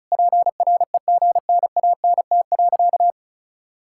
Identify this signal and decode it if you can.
Morse code, heard as a keyed tone, about 34 wpm, 700 Hz. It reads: PREGNANT.
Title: Sentences from Top 1000 Words - Review Part 6 - 35wpm